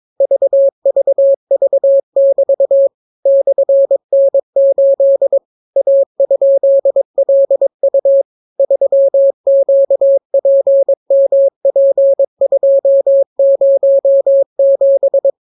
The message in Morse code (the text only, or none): VVV<BT> /N8 A?LU 3QPMP207